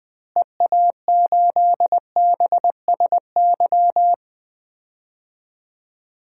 Morse code, heard as a keyed tone, about 20 wpm, 700 Hz. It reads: EA8BSY
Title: DXCC-100 Call Signs - 6 Characters - Repeated 2x - 20wpm